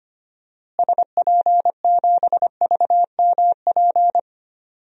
Morse code, heard as a keyed tone, about 25 wpm, 700 Hz. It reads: SP7VMP
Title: DXCC-100 Call Signs - 6 Characters - Repeated 3x - 25wpm